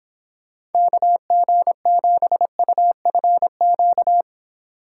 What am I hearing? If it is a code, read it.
Morse code, heard as a keyed tone, about 30 wpm, 700 Hz. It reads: KG7UFQ